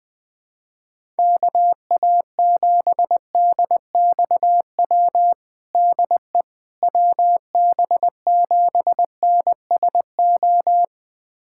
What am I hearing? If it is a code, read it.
Morse code, heard as a keyed tone, about 21 wpm, 700 Hz. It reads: KA7DXW DE WB7NSO
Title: USA Callsign DE Callsign 20wpm